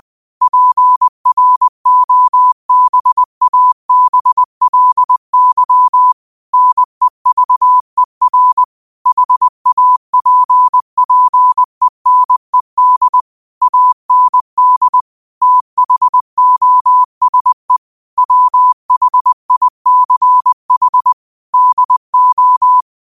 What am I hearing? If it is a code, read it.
Morse code, heard as a keyed tone, about 20 wpm, 1000 Hz. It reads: PROBABLY NEVER HAPPENED AND THOSE WHICH DO
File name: qotdfiles / 20wpm